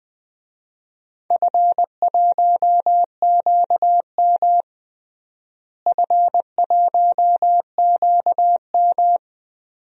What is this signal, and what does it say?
Morse code, heard as a keyed tone, about 21 wpm, 700 Hz. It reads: F1QM F1QM
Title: DXCC-100 Call Signs - 4 Characters - Repeated 2x - 20wpm